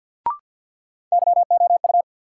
Morse code, heard as a keyed tone, about 48 wpm, 700 Hz. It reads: YOU
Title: Sentences from Top 100 Words - New Word Drill - 50wpm